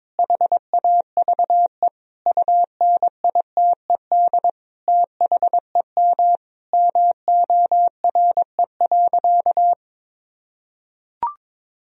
Morse code, heard as a keyed tone, about 22 wpm, 700 Hz. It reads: HAVE UNITED THEM MORE.
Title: Sentences from Top 1000 Words - Review Part 4 - 22wpm